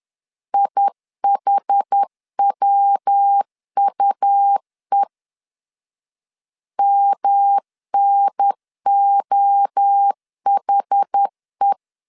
Morse code, heard as a keyed tone, about 10 wpm, 800 Hz. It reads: IHWUE MNOHE
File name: CW / DET / 1 3 1500C S B 10wpm 3335